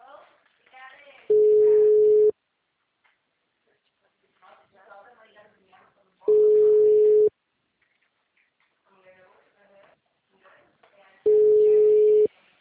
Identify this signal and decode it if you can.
Ringback tone (Europe, 425 Hz), with speech.